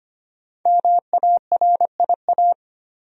Morse code, heard as a keyed tone, about 25 wpm, 700 Hz. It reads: MARIA
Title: USA Common Names - 25wpm